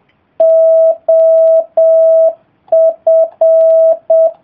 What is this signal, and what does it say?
Morse code, heard as a keyed tone, about 7 wpm, 660 Hz. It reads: OF